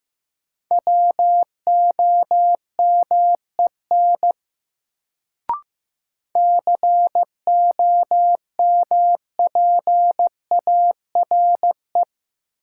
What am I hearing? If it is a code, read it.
Morse code, heard as a keyed tone, about 16 wpm, 700 Hz. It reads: WOMEN COMPARE
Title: Sentences from Top 1300 Words - New Word Drill Alt - 15wpm